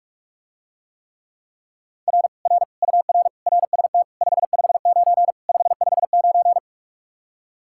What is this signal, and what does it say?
Morse code, heard as a keyed tone, about 49 wpm, 700 Hz. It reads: R R UR RST 559 559